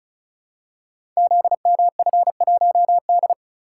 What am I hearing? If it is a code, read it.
Morse code, heard as a keyed tone, about 36 wpm, 700 Hz. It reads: ZMF1D
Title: Code Groups - 5 Letter:Numbers - 35wpm